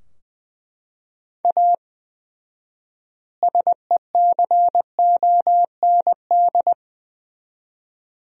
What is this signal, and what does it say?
Morse code, heard as a keyed tone, about 21 wpm, 700 Hz. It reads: A SECOND
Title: 4x Word-Spacing - Sentences from Top 300 Words - 20wpm